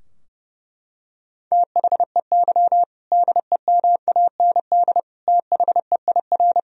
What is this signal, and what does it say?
Morse code, heard as a keyed tone, about 32 wpm, 700 Hz. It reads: THEY DEMAND THEIR
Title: Sentences from Top 1000 Words - Review Part 3 - 30wpm